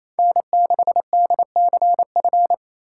Morse code, heard as a keyed tone, about 28 wpm, 700 Hz. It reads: N6DCF
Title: USA Call Signs - 5 Characters - 2x Character Spacing - 28wpm